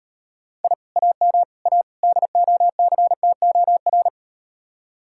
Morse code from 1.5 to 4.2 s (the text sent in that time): A DOCTOR